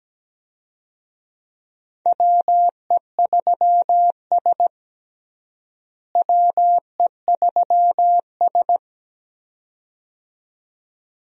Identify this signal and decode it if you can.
Morse code, heard as a keyed tone, about 17 wpm, 700 Hz. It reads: WE3S WE3S
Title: USA Call Signs - 4 Characters - Repeated 2x - 17wpm